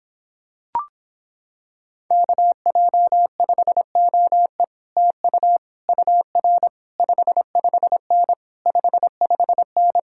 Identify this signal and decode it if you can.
Morse code, heard as a keyed tone, about 26 wpm, 700 Hz. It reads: KJ5OE TU UR 55N 55N